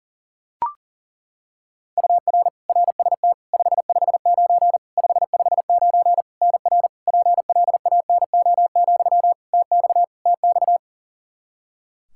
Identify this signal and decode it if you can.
Morse code, heard as a keyed tone, about 40 wpm, 700 Hz. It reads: UR RST 559 559 NR PLANO, TX TX
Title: Practice QSO - Exchange 1 - Beginning - 40wpm